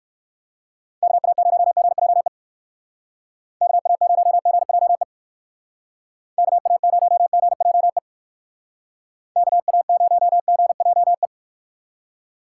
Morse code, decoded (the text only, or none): KA0GJE KA0GJE KA0GJE KA0GJE